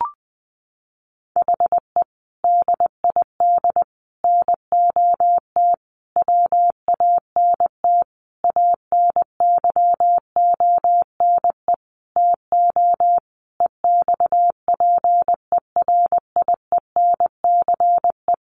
Morse code, heard as a keyed tone, about 20 wpm, 700 Hz. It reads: HE DID NOT WANT ANYONE TO EXPERIENCE
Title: Sentences from Top 1100 Words 20wpm